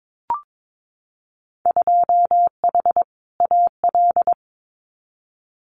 Morse code, heard as a keyed tone, about 22 wpm, 700 Hz. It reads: 2H AL
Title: Winter Field Day Exchanges 22wpm